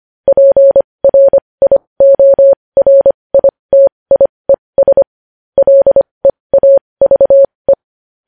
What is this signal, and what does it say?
Morse code, heard as a keyed tone, about 25 wpm, spaced as for 20 wpm, 550 Hz. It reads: PRIORITIES LEAVE
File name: podcast / News headlines-1x-25x20